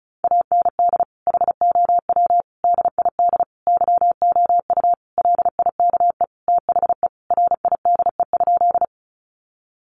Morse code, heard as a keyed tone, about 34 wpm, 700 Hz. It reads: AND HOW DID YOU LIKE THE RIDE?
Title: Sentences from Top 1200 Words 35wpm